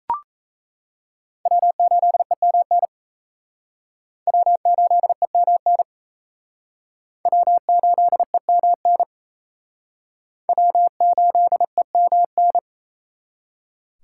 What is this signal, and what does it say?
Morse code, heard as a keyed tone, about 37 wpm, 700 Hz. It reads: W8EMN W8EMN W8EMN W8EMN